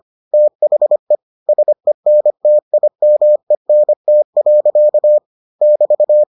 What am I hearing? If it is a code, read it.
Morse code, heard as a keyed tone, about 25 wpm, 600 Hz. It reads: THE SENTIMENT. <BT>